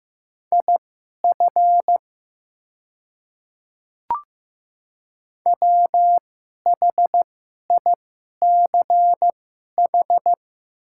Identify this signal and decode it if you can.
Morse code, heard as a keyed tone, about 15 wpm, spaced as for 11 wpm, 700 Hz. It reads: IF WHICH
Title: 2x Character Spacing - Top 100 Words - 15wpm